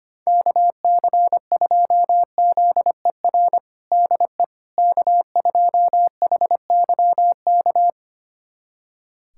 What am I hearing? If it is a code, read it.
Morse code, heard as a keyed tone, about 25 wpm, 700 Hz. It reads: KC2ZER DE K2HYK
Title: USA Callsign DE Callsign 25wpm